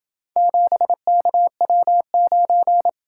Morse code, heard as a keyed tone, about 26 wpm, 700 Hz. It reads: 7KW9